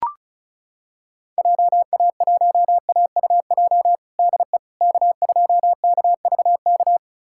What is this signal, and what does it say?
Morse code, heard as a keyed tone, about 34 wpm, 700 Hz. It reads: JA1AUJ DE K2KVK
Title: USA Callsign DE Callsign 35wpm